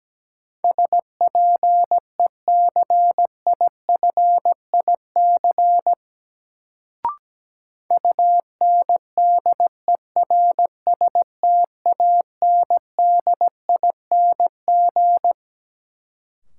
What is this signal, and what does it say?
Morse code, heard as a keyed tone, about 17 wpm, 700 Hz. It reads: SPECIFIC UNDERSTANDING
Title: Sentences from Top 1000 Words - Review Part 3 - New Word Drill - 17wpm